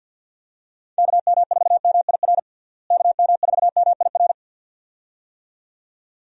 Morse code, heard as a keyed tone, about 50 wpm, 700 Hz. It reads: KM4MIR KM4MIR
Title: USA Call Signs - 6 Characters - Repeated 2x - 50wpm